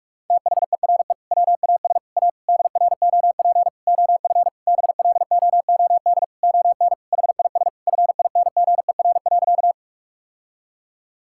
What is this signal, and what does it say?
Morse code, heard as a keyed tone, about 46 wpm, 700 Hz. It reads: THERE WAS A DROP OF BLOOD ON HIS FINGER.
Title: Sentences from Top 1600 Words 45wpm